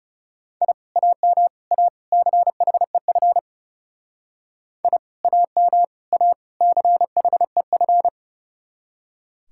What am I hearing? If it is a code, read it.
Morse code, heard as a keyed tone, about 38 wpm, 700 Hz. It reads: I AM A CHEF I AM A CHEF